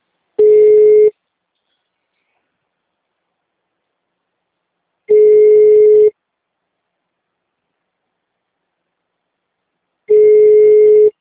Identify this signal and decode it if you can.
Ringback tone (Europe, 425 Hz).
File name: out- -2034- - - .24624